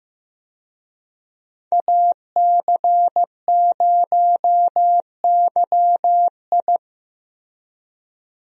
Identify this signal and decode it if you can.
Morse code, heard as a keyed tone, about 15 wpm, 700 Hz. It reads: AC0YI
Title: USA Call Signs - 5 Characters - Repeated 2x - 15wpm